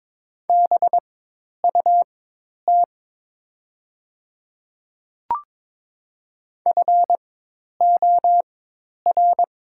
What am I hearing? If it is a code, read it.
Morse code, heard as a keyed tone, about 22 wpm, spaced as for 10 wpm, 690 Hz. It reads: BUT FOR
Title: 4x Character Spacing - Top 300 Words - 22wpm